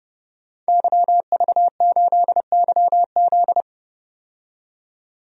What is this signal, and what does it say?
Morse code, heard as a keyed tone, about 30 wpm, 700 Hz. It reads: YV8YZ
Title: DXCC-100 Call Signs - 5 Characters - 30wpm